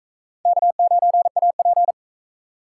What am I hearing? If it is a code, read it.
Morse code, heard as a keyed tone, about 38 wpm, 700 Hz. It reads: K9AP